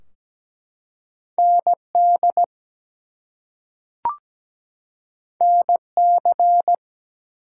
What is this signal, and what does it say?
Morse code, heard as a keyed tone, about 17 wpm, 700 Hz. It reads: ND NC
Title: US States and Canadian Provinces 17wpm